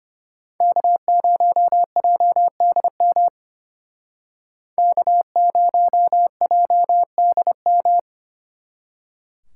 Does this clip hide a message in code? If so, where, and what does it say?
Yes: Morse code from 0.6 to 8.0 s, heard as a keyed tone, about 32 wpm, 700 Hz. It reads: K0JDM K0JDM